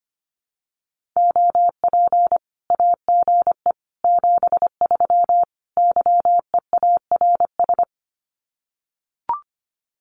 Morse code, heard as a keyed tone, about 25 wpm, 700 Hz. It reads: OP AGE 73 YEARS
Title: Common QSO Phrases 25wpm